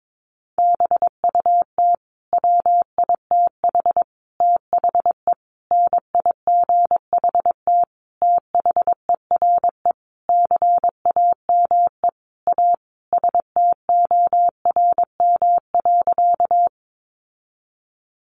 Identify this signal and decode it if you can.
Morse code, heard as a keyed tone, about 22 wpm, 700 Hz. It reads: BUT WITH THE NIGHT THERE CAME A STORM.
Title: Sentences from Top 1500 Words 22wpm